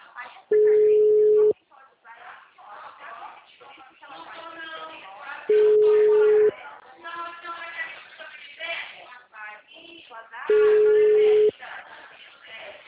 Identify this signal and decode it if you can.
Ringback tone (Europe, 425 Hz), with speech.
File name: out- -2016- - - .52667